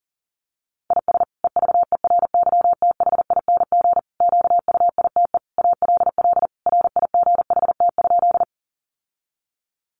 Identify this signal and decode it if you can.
Morse code, heard as a keyed tone, about 40 wpm, 700 Hz. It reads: IS EVERYTHING QUITE ALL RIGHT?
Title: Sentences from Top 700 Words 40wpm